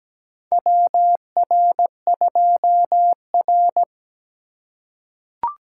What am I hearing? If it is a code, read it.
Morse code, heard as a keyed tone, about 17 wpm, 700 Hz. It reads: WR2R
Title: USA Call Signs - 4 Characters - 2x Character Spacing - 17wpm